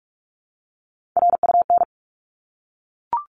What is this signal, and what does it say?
Morse code, heard as a keyed tone, about 44 wpm, 700 Hz. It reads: RUN
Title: Top 300 Words 45wpm